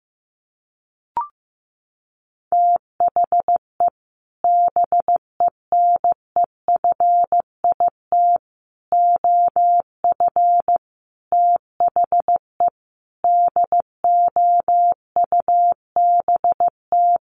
Morse code, heard as a keyed tone, about 15 wpm, 700 Hz. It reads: THE BENEFIT OF THE DOUBT
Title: Sets of 5 Words Encore 15wpm